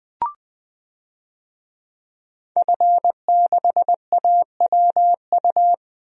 Morse code, heard as a keyed tone, about 20 wpm, 700 Hz. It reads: F6AWU
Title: DXCC-100 Call Signs - 5 Characters - 20wpm